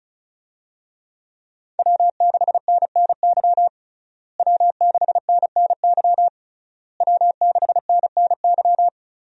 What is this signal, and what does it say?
Morse code, heard as a keyed tone, about 34 wpm, 700 Hz. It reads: W6NNY W6NNY W6NNY
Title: USA Call Signs - 5 Characters - Repeated 3x - 35wpm